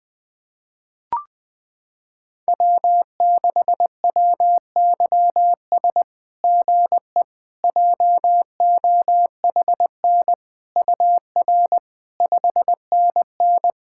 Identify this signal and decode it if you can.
Morse code, heard as a keyed tone, about 21 wpm, 700 Hz. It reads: W6WYS GE JOHN UR 5NN
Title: SOTA QSO Practice 20wpm